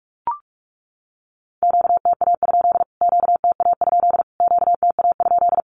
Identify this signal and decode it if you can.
Morse code, heard as a keyed tone, about 44 wpm, 700 Hz. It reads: QTA? QTA? QTA?
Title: Q-Codes Repeated 3x 45wpm